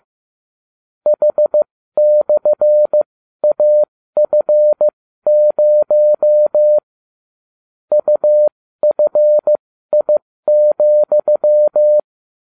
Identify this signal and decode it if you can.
Morse code, heard as a keyed tone, about 15 wpm, spaced as for 13 wpm, 600 Hz. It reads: H/AF0 UFI,